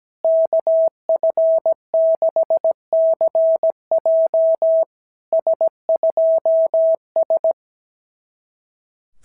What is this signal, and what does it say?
Morse code, heard as a keyed tone, about 17 wpm, 690 Hz. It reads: KF6CJ S2S
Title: SOTA QSO Practice 17wpm